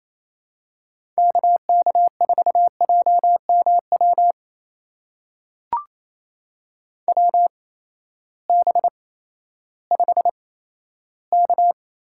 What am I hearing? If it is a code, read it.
Morse code, heard as a keyed tone, about 28 wpm, spaced as for 14 wpm, 700 Hz. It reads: KK4JMW WB5K
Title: USA Call Signs - 6 Characters - 8x Character Spacing - 28wpm